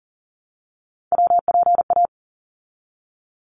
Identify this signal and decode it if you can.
Morse code, heard as a keyed tone, about 40 wpm, 700 Hz. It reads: WPA
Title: ARRL Sections 40wpm